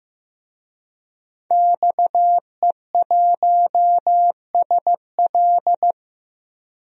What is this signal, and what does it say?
Morse code, heard as a keyed tone, about 15 wpm, 700 Hz. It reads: XE1SL
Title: DXCC-100 Call Signs - 5 Characters - Repeated 3x - 15wpm